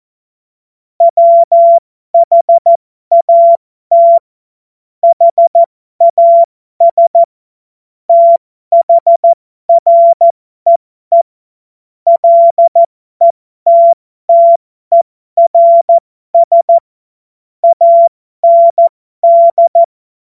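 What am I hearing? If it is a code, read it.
Morse code, heard as a keyed tone, about 14 wpm, spaced as for 12 wpm, 680 Hz. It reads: WHAT HAS THREE LETTERS AND
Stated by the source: Morse joke practice at effective 60, Farnsworth 70.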